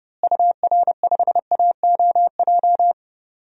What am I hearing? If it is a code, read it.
Morse code, heard as a keyed tone, about 30 wpm, 700 Hz. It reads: UR5AOJ
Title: DXCC-100 Call Signs - 6 Characters - Repeated 2x - 30wpm